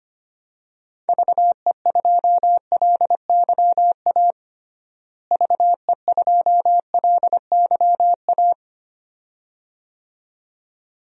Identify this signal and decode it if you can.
Morse code, heard as a keyed tone, about 25 wpm, 700 Hz. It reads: VE2LYA VE2LYA
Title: DXCC-100 Call Signs - 6 Characters - Repeated 2x - 25wpm